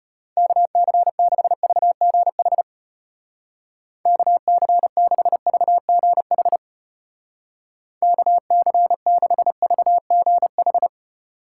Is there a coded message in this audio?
Yes: Morse code from 0.4 to 10.9 s, heard as a keyed tone, about 33 wpm, 700 Hz. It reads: KC6VGH KC6VGH KC6VGH